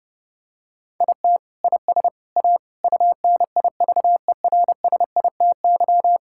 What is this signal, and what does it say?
Morse code, heard as a keyed tone, about 30 wpm, 700 Hz. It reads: IT IS A UNIVERSITY
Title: Sentences from Top 1000 Words - Review Part 5 - 30wpm